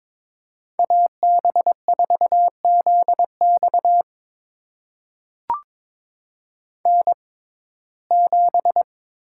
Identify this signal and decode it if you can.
Morse code, heard as a keyed tone, about 22 wpm, spaced as for 16 wpm, 700 Hz. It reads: AB4ZX N7
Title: USA Call Signs - 5 Characters - 6x Character Spacing - 22wpm